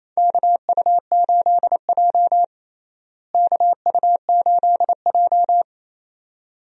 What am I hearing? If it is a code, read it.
Morse code, heard as a keyed tone, about 28 wpm, 700 Hz. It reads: KU8J KU8J
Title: USA Call Signs - 4 Characters - Repeated 2x - 28wpm